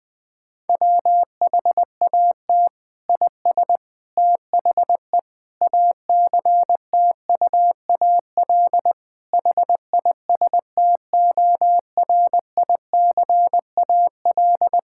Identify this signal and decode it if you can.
Morse code, heard as a keyed tone, about 20 wpm, 700 Hz. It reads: WHAT IS THE ACTUAL HISTORICAL
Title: Sentences from Top 1000 Words - Review Part 5 - 20wpm